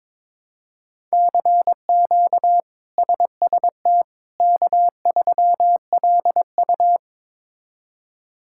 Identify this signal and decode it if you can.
Morse code, heard as a keyed tone, about 22 wpm, 700 Hz. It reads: CQ SST K3LU
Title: K1USN SST QSO Practice - 22wpm